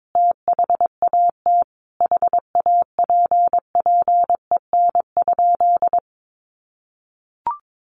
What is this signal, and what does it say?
Morse code, heard as a keyed tone, about 22 wpm, 700 Hz. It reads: THAT HAPPEN?
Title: Sentences from Top 400 Words 22wpm